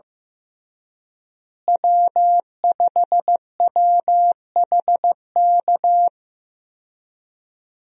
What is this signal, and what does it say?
Morse code, heard as a keyed tone, about 15 wpm, 700 Hz. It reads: W5WHK
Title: USA Call Signs - 5 Characters - Repeated 3x - 15wpm